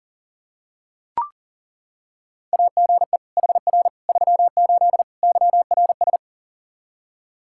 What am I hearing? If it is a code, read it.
Morse code, heard as a keyed tone, about 40 wpm, 700 Hz. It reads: AGE HR 38 YRS
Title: K1USN SST QSO Practice 40wpm